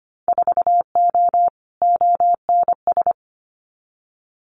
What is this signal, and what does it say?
Morse code, heard as a keyed tone, about 25 wpm, 700 Hz. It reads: 4O ONS
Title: Winter Field Day Exchanges 25wpm